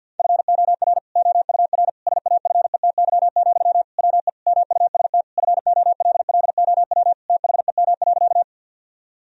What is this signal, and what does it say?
Morse code, heard as a keyed tone, about 50 wpm, 700 Hz. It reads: FOR OUR SAFETY, WE MUST FOLLOW THEM.
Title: Sentences from Top 1000 Words - Review Part 4 - 50wpm